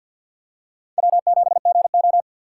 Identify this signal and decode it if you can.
Morse code, heard as a keyed tone, about 47 wpm, 700 Hz. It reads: W7GO
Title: USA Call Signs - 4 Characters - 3x Character Spacing - 50wpm